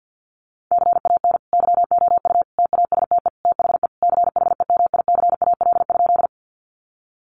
Morse code, heard as a keyed tone, about 50 wpm, 700 Hz. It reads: CAN YOU TASTE THE CHEMICAL?
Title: Sentences from Top 1200 Words 50wpm